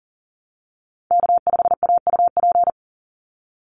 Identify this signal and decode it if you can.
Morse code, heard as a keyed tone, about 40 wpm, 700 Hz. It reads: K5AUP
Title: Call Signs Repeated 2x - 40wpm